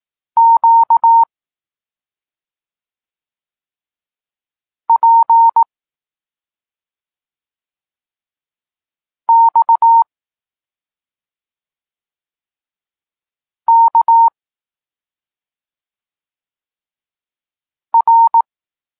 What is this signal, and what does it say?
Morse code, heard as a keyed tone, about 18 wpm, 920 Hz. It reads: Q P X K R
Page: Tekst 32 – Doskonalenie odbioru poznanych liter